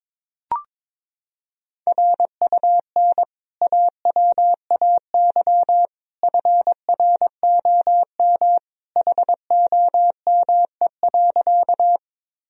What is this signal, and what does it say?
Morse code, heard as a keyed tone, about 21 wpm, 700 Hz. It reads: RUN AWAY FROM HOME.
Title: Sentences from Top 1000 Words - Review Part 1 - 22wpm